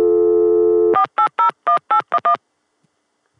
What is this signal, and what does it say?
DTMF, heard as telephone touch tones, tones of about 100 ms. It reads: 4801811